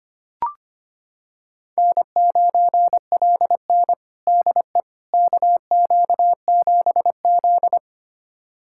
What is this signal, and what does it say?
Morse code, heard as a keyed tone, about 25 wpm, 700 Hz. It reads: N9LN DE KQ7Z
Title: USA Callsign DE Callsign - 4 Characters - 25wpm